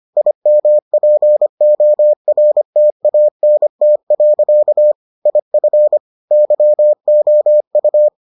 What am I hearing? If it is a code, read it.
Morse code, heard as a keyed tone, about 25 wpm, 600 Hz. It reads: IMPORTANT. IF YOU